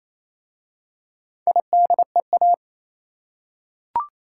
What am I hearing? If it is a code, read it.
Morse code, heard as a keyed tone, about 27 wpm, 700 Hz. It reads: IDEA
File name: Easy Words - Speed-Racing with Hints - 22wpm